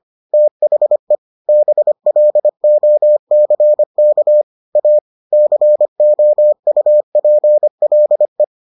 Morse code, heard as a keyed tone, about 25 wpm, 600 Hz. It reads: THE BLOCK A COUPLE